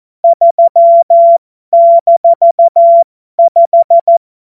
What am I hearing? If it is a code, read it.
Morse code, heard as a keyed tone, about 14 wpm, spaced as for 12 wpm, 680 Hz. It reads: 3-5
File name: Morsekurs / Witze